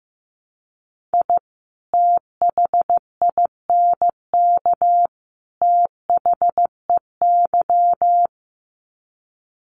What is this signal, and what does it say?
Morse code, heard as a keyed tone, about 15 wpm, 700 Hz. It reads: I THINK THEY
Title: Sets of 3 Words 15wpm